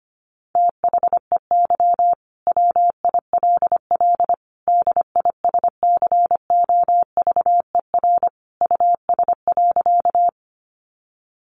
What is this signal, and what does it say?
Morse code, heard as a keyed tone, about 25 wpm, 690 Hz. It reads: THEY WILL DISCOVER US.
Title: Sentences from Top 1100 Words 25wpm